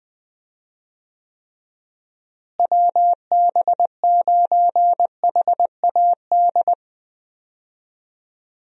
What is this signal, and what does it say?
Morse code, heard as a keyed tone, about 20 wpm, 700 Hz. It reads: WB9HAD
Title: USA Call Signs - 6 Characters - Repeated 3x - 20wpm